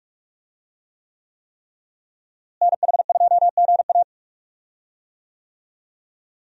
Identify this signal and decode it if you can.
Morse code, heard as a keyed tone, about 46 wpm, 700 Hz. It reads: NH2GA
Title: USA Call Signs - 5 Characters - Repeated 3x - 45wpm